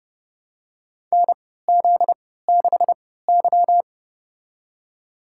Morse code, heard as a keyed tone, about 30 wpm, spaced as for 17 wpm, 700 Hz. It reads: NZ6Y
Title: USA Call Signs - 4 Characters - 3x Character Spacing - 30wpm